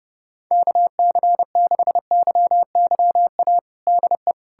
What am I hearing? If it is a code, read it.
Morse code, heard as a keyed tone, about 32 wpm, 700 Hz. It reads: KC6YYA DE
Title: USA Callsign DE Callsign - 6 Characters - 30wpm